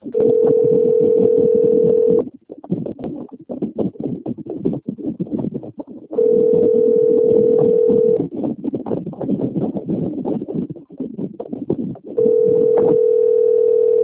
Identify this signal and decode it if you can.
Ringback tone (North America), 440 + 480 Hz, with noise.
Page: Dantas - PE